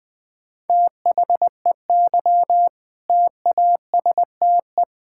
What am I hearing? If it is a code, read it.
Morse code, heard as a keyed tone, about 20 wpm, 700 Hz. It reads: THEY TASTE
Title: All-Age Appropriate Jokes - 20wpm